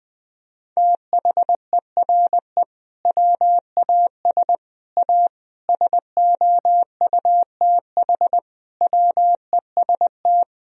Morse code, heard as a keyed tone, about 21 wpm, 700 Hz. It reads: THERE WAS A SOUTH WEST
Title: Sentences from Top 1000 Words - Review Part 2 - 20wpm